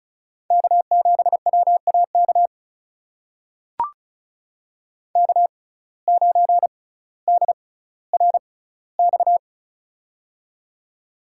Morse code, heard as a keyed tone, about 36 wpm, spaced as for 18 wpm, 700 Hz. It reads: K7WAK K9DRX
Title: USA Call Signs - 5 Characters - 6x Character Spacing - 35wpm